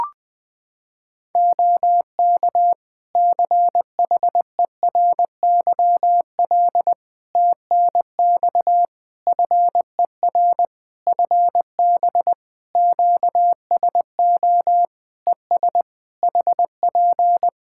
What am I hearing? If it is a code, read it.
Morse code, heard as a keyed tone, about 21 wpm, 700 Hz. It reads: OK CHERYL TNX FER FB QSO ES HP
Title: Practice QSO - Exchange 4 - Ending - 20wpm